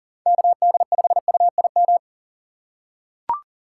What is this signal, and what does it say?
Morse code, heard as a keyed tone, about 40 wpm, 700 Hz. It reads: KD5UIM
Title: USA Call Signs - 6 Characters - 3x Character Spacing - 40wpm